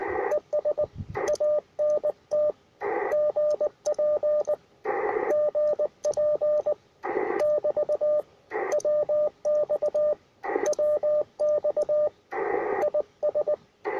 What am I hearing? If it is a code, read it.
Morse code, heard as a keyed tone, about 19 wpm, 600 Hz, with radio noise. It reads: ES ANT GP GP <BT> WX WX IS
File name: attachments / R7AO UR4MCK